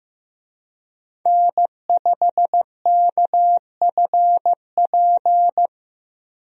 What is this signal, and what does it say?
Morse code, heard as a keyed tone, about 15 wpm, 700 Hz. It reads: N5KFP
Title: USA Call Signs - 5 Characters - 15wpm